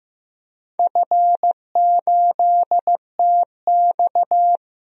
Morse code, heard as a keyed tone, about 15 wpm, 700 Hz. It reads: F8TX
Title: DXCC-100 Call Signs - 4 Characters - Repeated 2x - 15wpm